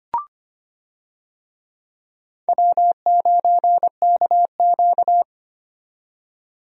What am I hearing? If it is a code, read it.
Morse code, heard as a keyed tone, about 25 wpm, 700 Hz. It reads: W9KQ
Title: USA Call Signs - 4 Characters - 25wpm